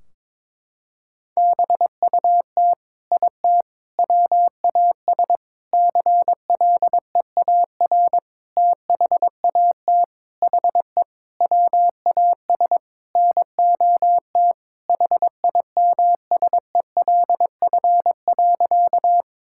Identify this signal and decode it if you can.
Morse code, heard as a keyed tone, about 22 wpm, 700 Hz. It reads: BUT IT WAS CLEAR THAT HE WAS NOT HIMSELF.
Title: Sentences from Top 1600 Words 22wpm